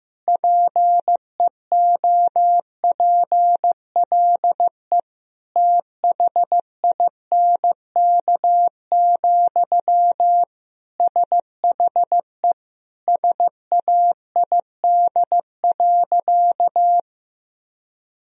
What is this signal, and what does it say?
Morse code, heard as a keyed tone, about 15 wpm, 700 Hz. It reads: PEOPLE THINK, SHE SAID.
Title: Sentences from Top 1000 Words - Review Part 4 - 15wpm